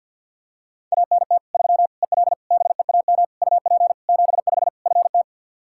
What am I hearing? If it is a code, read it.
Morse code, heard as a keyed tone, about 49 wpm, 700 Hz. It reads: ANT 3 EL BEAM UP 75 FT
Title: Practice QSO - Exchange 2 - Info - 50wpm